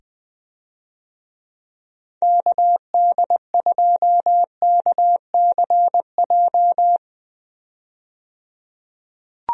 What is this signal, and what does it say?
Morse code, heard as a keyed tone, about 20 wpm, 700 Hz. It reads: KD2KCJ
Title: USA Call Signs - 6 Characters - 20wpm